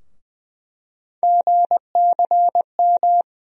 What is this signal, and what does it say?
Morse code, heard as a keyed tone, about 20 wpm, 700 Hz. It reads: GCM
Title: USA Call Sign Postfixes - 3 Letters 20wpm